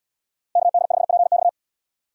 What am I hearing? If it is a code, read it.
Morse code, heard as a keyed tone, about 70 wpm, 700 Hz. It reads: KD5PX